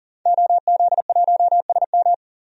Morse code, heard as a keyed tone, about 40 wpm, 700 Hz. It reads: OZ1SM
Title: DXCC-100 Call Signs - 5 Characters - 40wpm